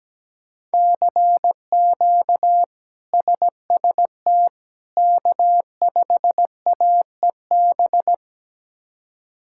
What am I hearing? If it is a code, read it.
Morse code, heard as a keyed tone, about 17 wpm, 700 Hz. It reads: CQ SST K5AEB
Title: K1USN SST QSO Practice - 17wpm